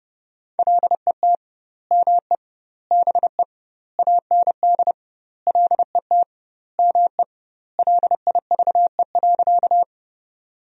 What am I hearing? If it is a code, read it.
Morse code, heard as a keyed tone, about 30 wpm, 700 Hz. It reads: LET ME BE AND LET ME LIVE.
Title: 2x Word-Spacing - Sentences from Top 300 Words - 30wpm